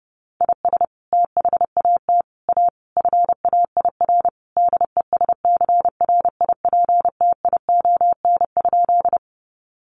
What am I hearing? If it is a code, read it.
Morse code, heard as a keyed tone, about 30 wpm, 700 Hz. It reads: IS THAT A FAIR DESCRIPTION?
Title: Sentences from Top 1300 Words 30wpm